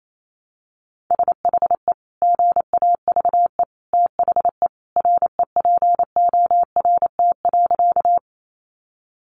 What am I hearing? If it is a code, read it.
Morse code, heard as a keyed tone, about 28 wpm, 690 Hz. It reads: SHE GAVE THE REPORT.
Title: Sentences from Top 900 Words 28wpm